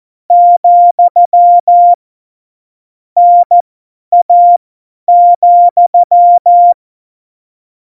Morse code, heard as a keyed tone, about 14 wpm, spaced as for 10 wpm, 700 Hz. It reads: , NA,
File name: Morsekurs / Witze / Witz 21 Juli 2020, 50 F70